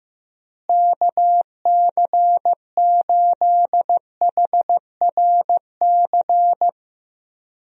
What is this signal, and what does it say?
Morse code, heard as a keyed tone, about 15 wpm, 700 Hz. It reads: KC8HRC
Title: USA Call Signs - 6 Characters - Repeated 3x - 15wpm